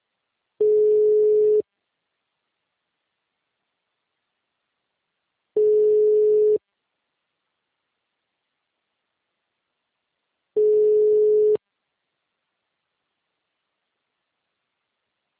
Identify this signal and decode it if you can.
Ringback tone (Europe, 425 Hz).
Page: Itu